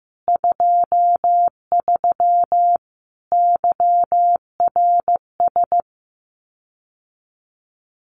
Morse code, heard as a keyed tone, about 15 wpm, 700 Hz. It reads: 23 YRS
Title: Common QSO Phrases Encore 15wpm